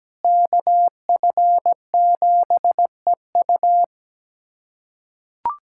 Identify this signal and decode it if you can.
Morse code, heard as a keyed tone, about 17 wpm, 700 Hz. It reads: KF7EU
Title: USA Callsign DE Callsign - 5 Characters - 17wpm